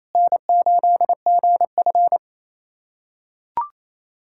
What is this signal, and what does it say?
Morse code, heard as a keyed tone, about 28 wpm, 700 Hz. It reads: N8GF
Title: USA Callsign DE Callsign 28wpm